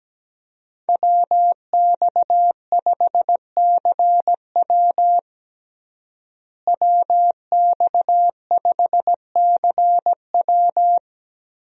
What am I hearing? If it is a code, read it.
Morse code, heard as a keyed tone, about 17 wpm, 700 Hz. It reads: WX5CW WX5CW
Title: USA Call Signs - 5 Characters - Repeated 2x - 17wpm